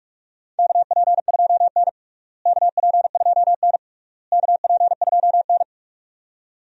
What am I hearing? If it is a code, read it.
Morse code, heard as a keyed tone, about 46 wpm, 700 Hz. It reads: KP2N KP2N KP2N
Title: DXCC-100 Call Signs - 4 Characters - Repeated 3x - 45wpm